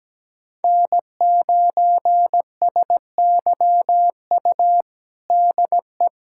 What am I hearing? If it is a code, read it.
Morse code, heard as a keyed tone, about 17 wpm, 700 Hz. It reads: N9SYU DE
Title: USA Callsign DE Callsign 17wpm